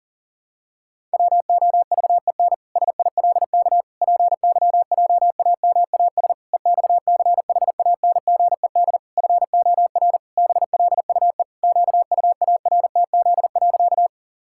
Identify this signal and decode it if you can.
Morse code, heard as a keyed tone, about 40 wpm, 700 Hz. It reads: WOVEN SILK PYJAMAS EXCHANGED FOR BLUE QUARTZ.
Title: Pangram Challenge v2 - 40wpm